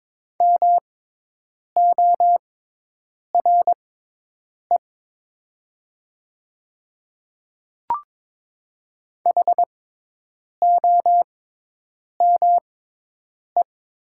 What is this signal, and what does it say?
Morse code, heard as a keyed tone, about 22 wpm, spaced as for 8 wpm, 700 Hz. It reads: MORE HOME
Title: 6x Character Spacing - Top 300 Words - 22wpm